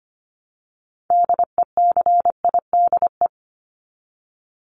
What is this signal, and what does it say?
Morse code, heard as a keyed tone, about 25 wpm, 700 Hz. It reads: DECIDE
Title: Top 500 Words 25wpm